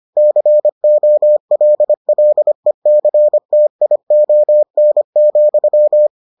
Morse code, heard as a keyed tone, about 25 wpm, 600 Hz. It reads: COLLECTION,